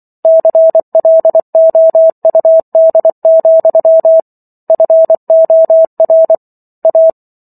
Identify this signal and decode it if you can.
Morse code, heard as a keyed tone, about 24 wpm, 650 Hz. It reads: CLOUD, FOR A